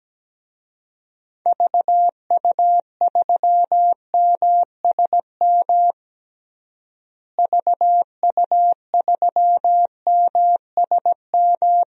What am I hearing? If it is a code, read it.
Morse code, heard as a keyed tone, about 17 wpm, 700 Hz. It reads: VU3MSM VU3MSM
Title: DXCC-100 Call Signs - 6 Characters - Repeated 3x - 17wpm